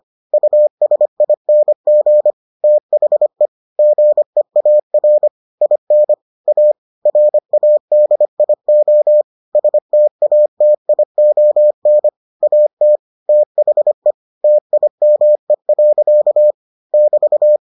Morse code, heard as a keyed tone, about 25 wpm, 600 Hz. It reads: USING THE GEAR IN A RADIO STATION AT THE TIME. <BT>